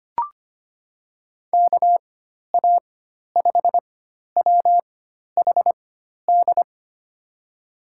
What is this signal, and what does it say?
Morse code, heard as a keyed tone, about 25 wpm, spaced as for 12 wpm, 700 Hz. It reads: KA5WHD
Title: USA Call Signs - 6 Characters - 4x Character Spacing - 25wpm